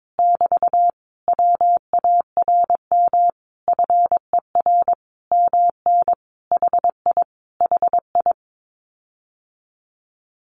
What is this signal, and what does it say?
Morse code, heard as a keyed tone, about 22 wpm, 700 Hz. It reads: <BT> WARM FER MN HI HI
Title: Common QSO Phrases Encore 22wpm